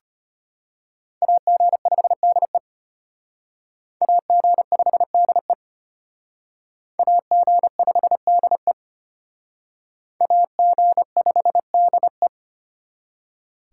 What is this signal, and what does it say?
Morse code, heard as a keyed tone, about 33 wpm, 700 Hz. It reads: AG5DE AG5DE AG5DE AG5DE